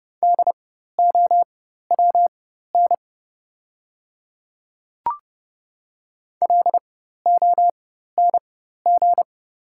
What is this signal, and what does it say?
Morse code, heard as a keyed tone, about 31 wpm, spaced as for 14 wpm, 700 Hz. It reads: DOWN LONG
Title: 4x Character Spacing - Top 100 Words - 30wpm